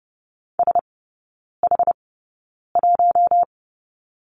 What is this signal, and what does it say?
Morse code, heard as a keyed tone, about 30 wpm, spaced as for 9 wpm, 700 Hz. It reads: SH1
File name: Lesson 18e - Letter m - Three Characters 30wpm